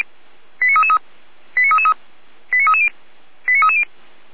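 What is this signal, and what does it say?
Five-tone selective call (ZVEI), heard as quick runs of five tones, tones of about 70 ms. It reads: 89333 89333 89330 89330
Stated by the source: ZVEI five-tone selective call received on an SDR.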